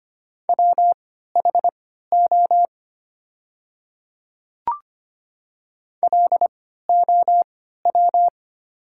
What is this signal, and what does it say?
Morse code, heard as a keyed tone, about 25 wpm, spaced as for 14 wpm, 690 Hz. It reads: WHO LOW
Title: 3x Character Spacing - Top 300 Words - 25wpm